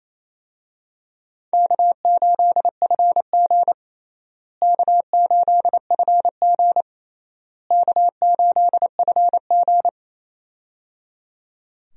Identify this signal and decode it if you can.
Morse code, heard as a keyed tone, about 27 wpm, 700 Hz. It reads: K8FG K8FG K8FG
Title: USA Call Signs - 4 Characters - Repeated 3x - 28wpm